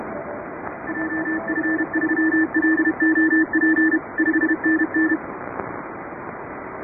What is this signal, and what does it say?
Morse code, heard as a keyed tone, about 31 wpm, 1750 Hz, with radio noise. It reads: JF3LOP 5NN
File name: chatham / 2350-12CW